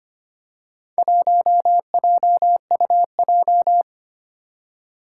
Morse code, heard as a keyed tone, about 25 wpm, 700 Hz. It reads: 1JUJ
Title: Four Challenging Letter-Number Combinations 25wpm